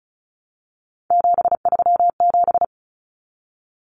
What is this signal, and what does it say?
Morse code, heard as a keyed tone, about 34 wpm, 700 Hz. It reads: 737
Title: Common Cut Numbers - 3-Digit 35wpm